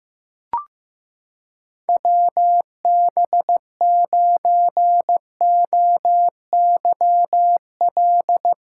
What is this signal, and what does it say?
Morse code, heard as a keyed tone, about 15 wpm, 700 Hz. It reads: WB9OYL
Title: USA Callsign DE Callsign - 6 Characters - 15wpm